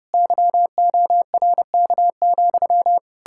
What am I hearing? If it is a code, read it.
Morse code, heard as a keyed tone, about 40 wpm, 700 Hz. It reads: YORK,